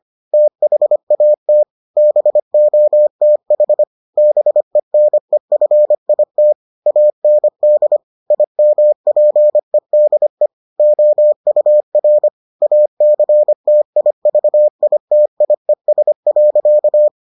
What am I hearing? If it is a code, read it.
Morse code, heard as a keyed tone, about 25 wpm, 600 Hz. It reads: THAT BOTH BENEFIT AND IMPEDE OUR ACTIVITIES.